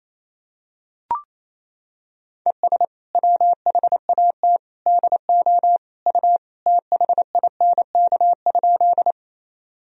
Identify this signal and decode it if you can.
Morse code, heard as a keyed tone, about 28 wpm, 700 Hz. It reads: ES WHAT DO U THINK?
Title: Sentences from Top 200 Words and Abbreviations 28wpm